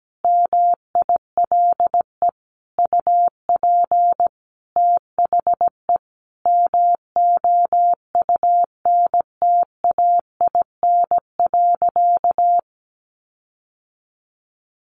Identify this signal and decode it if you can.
Morse code, heard as a keyed tone, about 17 wpm, 700 Hz. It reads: MILE UP THE MOUNTAIN.
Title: Sentences from Top 1300 Words 17wpm